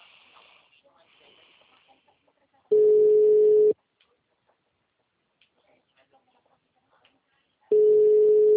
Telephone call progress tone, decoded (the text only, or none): ringback tone